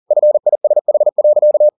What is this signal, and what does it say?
Morse code, heard as a keyed tone, about 40 wpm, 600 Hz. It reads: FISH.